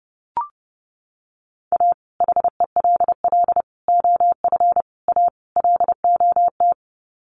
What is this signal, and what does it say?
Morse code, heard as a keyed tone, about 30 wpm, 700 Hz. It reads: A HELL OF A LOT
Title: Sets of 5 Words Encore 30wpm